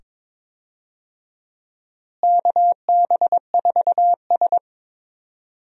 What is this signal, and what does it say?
Morse code, heard as a keyed tone, about 23 wpm, 700 Hz. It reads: KB4S
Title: USA Call Signs - 4 Characters - Repeated 2x - 22wpm